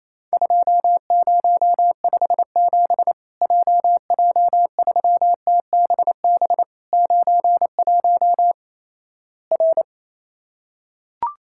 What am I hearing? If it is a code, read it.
Morse code, heard as a keyed tone, about 28 wpm, 700 Hz. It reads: 2057 JJ3TBB 91 R
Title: Worked All Europe DX Contest - QTC Exchange - 28wpm